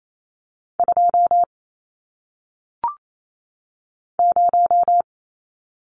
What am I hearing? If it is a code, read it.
Morse code, heard as a keyed tone, about 28 wpm, 700 Hz. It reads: 2 0
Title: Numbers - 1-Digit 28wpm